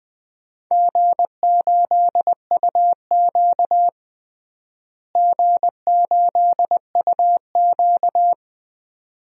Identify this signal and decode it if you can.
Morse code, heard as a keyed tone, about 20 wpm, 700 Hz. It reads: G8UQ G8UQ
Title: DXCC-100 Call Signs - 4 Characters - Repeated 2x - 20wpm